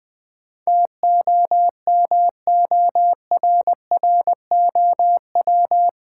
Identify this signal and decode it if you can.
Morse code, heard as a keyed tone, about 20 wpm, 700 Hz. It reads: TOMORROW
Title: Sentences from Top 1100 Words - New Word Drill Alt - 20wpm